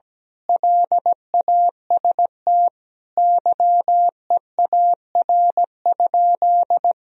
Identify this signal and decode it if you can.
Morse code, heard as a keyed tone, about 17 wpm, 700 Hz. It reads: LAST YEAR?
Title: Sentences from Top 1000 Words - Review Part 4 - 17wpm